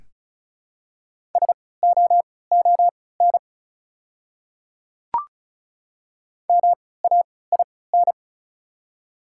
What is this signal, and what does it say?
Morse code, heard as a keyed tone, about 36 wpm, spaced as for 20 wpm, 690 Hz. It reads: SOON MAIN
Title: 3x Character Spacing - Top 500 Words - 35wpm